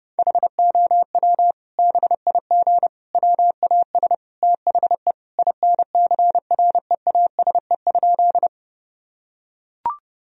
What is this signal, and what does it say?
Morse code, heard as a keyed tone, about 30 wpm, 700 Hz. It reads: HOW BIG WAS THE INCREASE?
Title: Sentences from Top 1000 Words - Review Part 4 - 30wpm